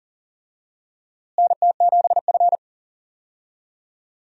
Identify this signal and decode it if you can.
Morse code, heard as a keyed tone, about 39 wpm, 700 Hz. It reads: NT7F
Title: USA Call Signs - 4 Characters - Repeated 3x - 40wpm